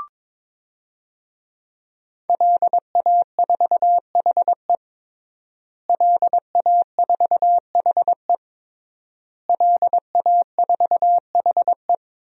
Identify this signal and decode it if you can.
Morse code, heard as a keyed tone, about 22 wpm, 700 Hz. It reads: LA4HE LA4HE LA4HE
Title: DXCC-100 Call Signs - 5 Characters - Repeated 3x - 22wpm